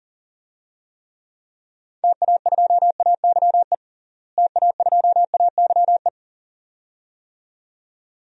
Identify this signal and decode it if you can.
Morse code, heard as a keyed tone, about 40 wpm, 700 Hz. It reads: TA2AYE TA2AYE
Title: DXCC-100 Call Signs - 6 Characters - Repeated 2x - 40wpm